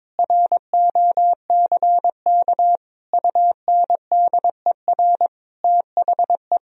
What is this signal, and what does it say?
Morse code, heard as a keyed tone, about 22 wpm, 700 Hz. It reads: ROCK UNDER THE
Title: Sentences from Top 1000 Words - Review Part 2 - 22wpm